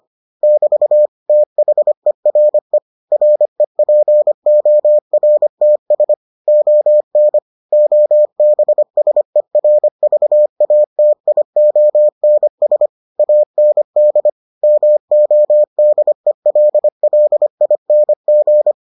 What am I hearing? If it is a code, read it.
Morse code, heard as a keyed tone, about 25 wpm, 600 Hz. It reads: <BT> THERE REPORTS ON OBSERVATIONS AND MODELLING